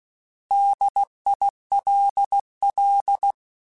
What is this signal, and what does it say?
Morse code, heard as a keyed tone, about 16 wpm, 800 Hz. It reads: DILL